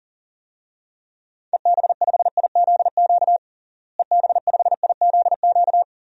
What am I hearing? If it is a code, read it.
Morse code, heard as a keyed tone, about 37 wpm, 700 Hz. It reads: EB5IZQ EB5IZQ
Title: DXCC-100 Call Signs - 6 Characters - Repeated 3x - 40wpm